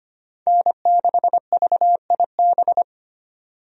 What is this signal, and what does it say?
Morse code, heard as a keyed tone, about 25 wpm, 700 Hz. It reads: N6VIB
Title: USA Call Signs - 5 Characters - 25wpm